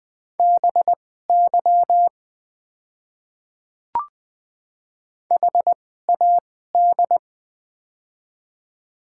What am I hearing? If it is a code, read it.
Morse code, heard as a keyed tone, about 20 wpm, spaced as for 14 wpm, 700 Hz. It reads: BY HAD
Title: 2x Character Spacing - Top 200 Words - 20wpm